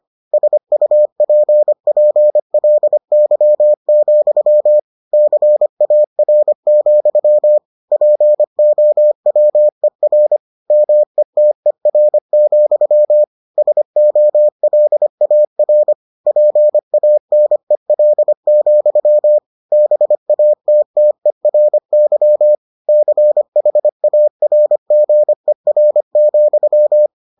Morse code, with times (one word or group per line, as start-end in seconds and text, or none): 0.3-4.8 SUPPLY,
5.1-7.6 CAR,
7.9-10.4 POWER
10.7-13.2 METER,
13.6-15.9 SOLAR
16.3-19.4 PANEL,
19.7-22.5 BATTERY
22.9-27.1 CHARGER,